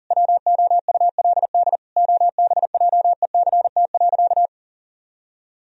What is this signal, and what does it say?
Morse code, heard as a keyed tone, about 40 wpm, 700 Hz. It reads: WOULD OBJECT.
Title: Sentences from Top 1000 Words - Review Part 3 - 40wpm